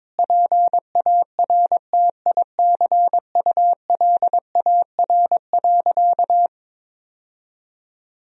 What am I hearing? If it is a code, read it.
Morse code, heard as a keyed tone, about 22 wpm, 700 Hz. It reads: PARTICULAR.
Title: Sentences from Top 1000 Words - Review Part 4 - 22wpm